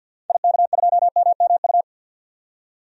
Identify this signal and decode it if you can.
Morse code, heard as a keyed tone, about 48 wpm, 700 Hz. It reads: IK2MMU
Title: DXCC-100 Call Signs - 6 Characters - 50wpm